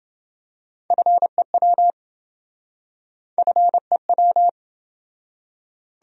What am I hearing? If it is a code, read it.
Morse code, heard as a keyed tone, about 28 wpm, 700 Hz. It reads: FEW FEW